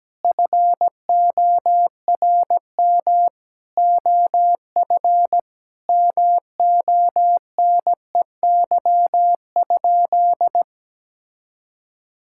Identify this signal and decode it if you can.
Morse code, heard as a keyed tone, about 17 wpm, 700 Hz. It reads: FORM OF MONEY?
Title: Sentences from Top 1000 Words - Review Part 6 - 17wpm